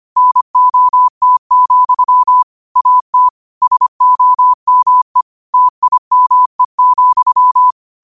Morse code, heard as a keyed tone, about 25 wpm, 1000 Hz. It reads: NOT, AT SOME TIME,
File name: qotdfiles / 25wpm